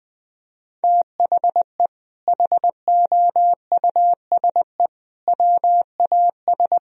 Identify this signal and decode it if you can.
Morse code, heard as a keyed tone, about 21 wpm, 700 Hz. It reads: THE HOUSE WAS
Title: Sentences from Top 1000 Words - Review Part 2 - 20wpm